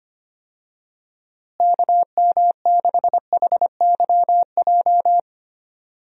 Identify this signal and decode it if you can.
Morse code, heard as a keyed tone, about 25 wpm, 700 Hz. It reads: KM6HYJ
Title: USA Call Signs - 6 Characters - Repeated 3x - 25wpm